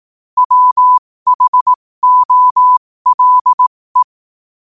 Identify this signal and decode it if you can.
Morse code, heard as a keyed tone, about 18 wpm, spaced as for 15 wpm, 1000 Hz. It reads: WHOLE
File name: qotdfiles / 15wpm